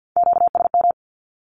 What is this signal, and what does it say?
Morse code, heard as a keyed tone, about 50 wpm, 700 Hz. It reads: QSM
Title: Q-Codes with Spoken Meaning - 50wpm